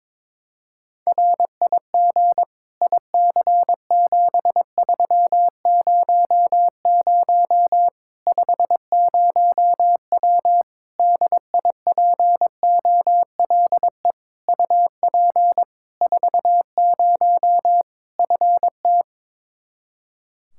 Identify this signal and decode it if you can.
Morse code, heard as a keyed tone, about 21 wpm, 700 Hz. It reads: RIG IC7300 50W DIPOLE UP 40 FT
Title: Practice QSO - Exchange 2 - Info - 22wpm